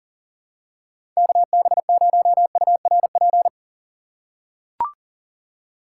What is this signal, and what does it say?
Morse code, heard as a keyed tone, about 38 wpm, 700 Hz. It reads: KB0URP
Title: USA Call Signs - 6 Characters - 8x Character Spacing - 40wpm